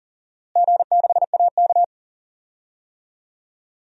Morse code, heard as a keyed tone, about 40 wpm, 700 Hz. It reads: G6AK
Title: DXCC-100 Call Signs - 4 Characters - Repeated 3x - 40wpm